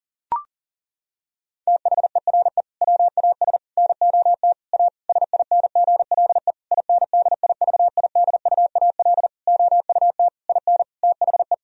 Morse code, heard as a keyed tone, about 37 wpm, 700 Hz. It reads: THERE WAS NOT A SINGLE INDIVIDUAL OUT IN THE
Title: Sentences from Top 1000 Words - Review Part 4 - 40wpm